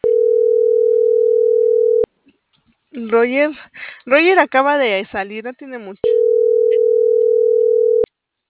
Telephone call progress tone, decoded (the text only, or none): ringback tone